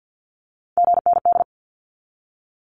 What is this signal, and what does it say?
Morse code, heard as a keyed tone, about 50 wpm, 700 Hz. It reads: GND
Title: Abbreviations 50wpm